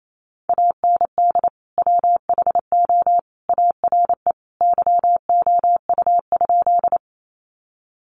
Morse code, heard as a keyed tone, about 28 wpm, 700 Hz. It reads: AND WHO ARE YOU?
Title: Sentences from Top 100 Words 28wpm